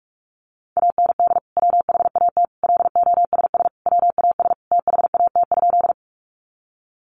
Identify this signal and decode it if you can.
Morse code, heard as a keyed tone, about 44 wpm, 700 Hz. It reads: AND WHAT LOSS WAS THAT?
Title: Sentences from Top 1400 Words 45wpm